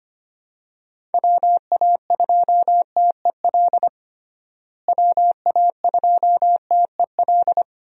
Morse code, heard as a keyed tone, about 25 wpm, 700 Hz. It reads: WA2TEL WA2TEL
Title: USA Call Signs - 6 Characters - Repeated 2x - 25wpm